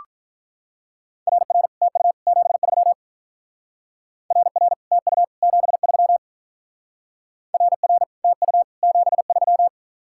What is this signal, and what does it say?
Morse code, heard as a keyed tone, about 44 wpm, 700 Hz. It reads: RR TU 73 RR TU 73 RR TU 73